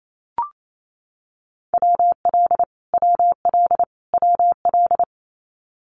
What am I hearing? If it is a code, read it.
Morse code, heard as a keyed tone, about 28 wpm, 700 Hz. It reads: WL WL WL
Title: Abbreviations Repeated 3x - 28wpm